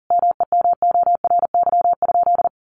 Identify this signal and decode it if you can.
Morse code, heard as a keyed tone, about 37 wpm, 700 Hz. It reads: MEMORY?
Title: Sentences from Top 1300 Words 40wpm